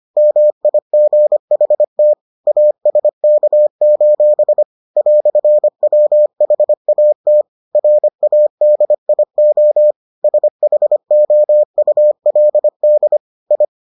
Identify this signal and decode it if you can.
Morse code, heard as a keyed tone, about 25 wpm, 600 Hz. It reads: MIGHT ASK: "WHAT RADIO SHOULD I